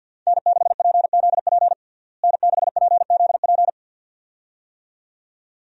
Morse code, heard as a keyed tone, about 46 wpm, 700 Hz. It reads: N6PZP N6PZP
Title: USA Call Signs - 5 Characters - Repeated 2x - 50wpm